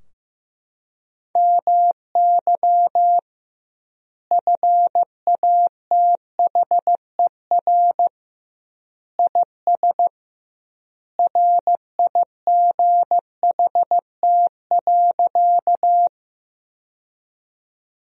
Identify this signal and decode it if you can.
Morse code, heard as a keyed tone, about 15 wpm, 700 Hz. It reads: MY FATHER IS RIGHT.
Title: 2x Word-Spacing - Sentences from Top 300 Words - 15wpm